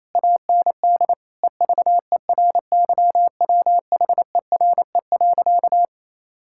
Morse code, heard as a keyed tone, about 28 wpm, 700 Hz. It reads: AND EVERYWHERE.
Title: Sentences from Top 1600 Words 28wpm